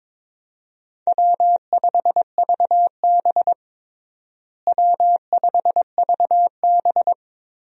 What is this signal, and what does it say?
Morse code, heard as a keyed tone, about 22 wpm, 700 Hz. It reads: W5VB W5VB
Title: USA Call Signs - 4 Characters - Repeated 2x - 22wpm